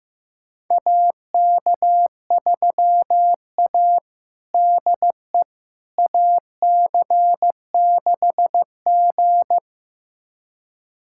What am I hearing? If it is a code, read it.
Morse code, heard as a keyed tone, about 15 wpm, 700 Hz. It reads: AK3A DE AC6G
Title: USA Callsign DE Callsign - 4 Characters - 15wpm